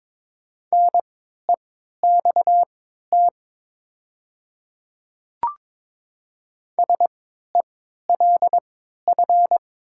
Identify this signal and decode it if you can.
Morse code, heard as a keyed tone, about 22 wpm, spaced as for 12 wpm, 700 Hz. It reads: NEXT SELF
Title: 3x Character Spacing - Top 300 Words - 22wpm